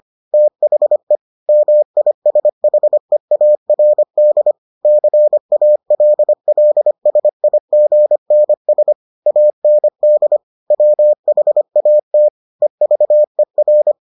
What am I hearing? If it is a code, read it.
Morse code, heard as a keyed tone, about 25 wpm, 600 Hz. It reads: THE MISHEARD CALLSIGNS AND WHAT EVER